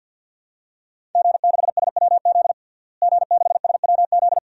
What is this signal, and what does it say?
Morse code, heard as a keyed tone, about 47 wpm, 700 Hz. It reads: G6SWZ G6SWZ
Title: DXCC-100 Call Signs - 5 Characters - Repeated 2x - 50wpm